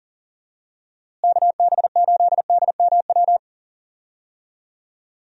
Morse code, heard as a keyed tone, about 40 wpm, 700 Hz. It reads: KB8DMW
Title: USA Call Signs - 6 Characters - Repeated 2x - 40wpm